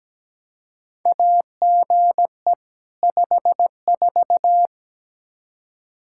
Morse code, heard as a keyed tone, about 17 wpm, 700 Hz. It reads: AGE 54
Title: Practice QSO - Exchange 3 - More Info - 17wpm